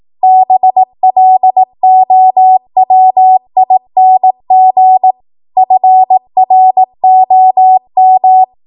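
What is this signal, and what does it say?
Morse code, heard as a keyed tone, about 18 wpm, 750 Hz. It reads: BLOWING FROM
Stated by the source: Code practice at 18 WPM, a transition run bridging 15 to 20 WPM.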